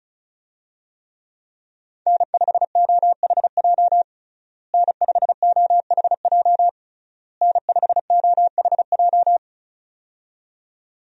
Morse code, heard as a keyed tone, about 34 wpm, 700 Hz. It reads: N5OHJ N5OHJ N5OHJ
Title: USA Call Signs - 5 Characters - Repeated 3x - 35wpm